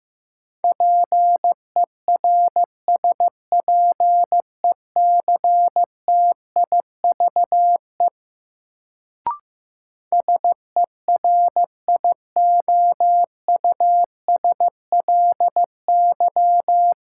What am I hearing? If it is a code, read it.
Morse code, heard as a keyed tone, about 15 wpm, 700 Hz. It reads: PERSPECTIVE SERIOUSLY
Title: Sentences from Top 1200 Words - New Word Drill Alt - 15wpm